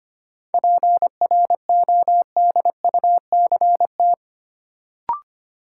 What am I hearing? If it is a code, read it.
Morse code, heard as a keyed tone, about 25 wpm, 700 Hz. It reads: PRODUCT
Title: Sentences from Top 1400 Words - New Word Drill Alt - 25wpm